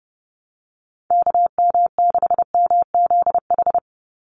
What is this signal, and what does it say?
Morse code, heard as a keyed tone, about 30 wpm, 700 Hz. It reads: KM6MZH
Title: USA Call Signs 30wpm